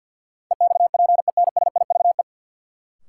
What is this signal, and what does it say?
Morse code, heard as a keyed tone, about 49 wpm, 700 Hz. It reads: EXPENSIVE
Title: Sentences from Top 1500 Words - New Word Drill Alt - 50wpm